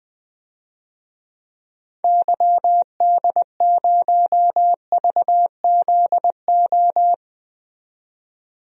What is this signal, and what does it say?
Morse code, heard as a keyed tone, about 20 wpm, 700 Hz. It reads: YD0VZO
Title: DXCC-100 Call Signs - 6 Characters - 20wpm